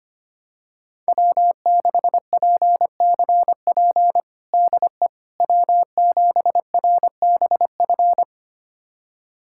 Morse code, heard as a keyed tone, about 25 wpm, 700 Hz. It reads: W6PCP DE W7RBF
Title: USA Callsign DE Callsign - 5 Characters - 25wpm